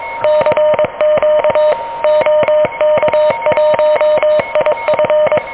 Morse code, heard as a keyed tone, about 22 wpm, 600 Hz, with noise. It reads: CQ OK1IF
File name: OK QRP 2023 / OK QRP 2023